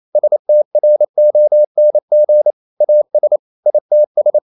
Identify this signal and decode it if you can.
Morse code, heard as a keyed tone, about 28 wpm, 600 Hz. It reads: STRONG AS ITS
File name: CW / Practice / EnglishSayings28